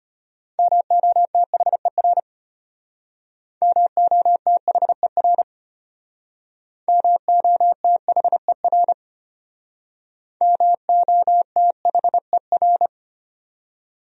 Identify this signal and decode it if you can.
Morse code, heard as a keyed tone, about 33 wpm, 700 Hz. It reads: MOTHER MOTHER MOTHER MOTHER